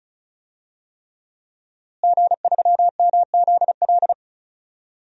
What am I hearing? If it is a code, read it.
Morse code, heard as a keyed tone, about 36 wpm, 700 Hz. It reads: G3MZL
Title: DXCC-100 Call Signs - 5 Characters - Repeated 3x - 35wpm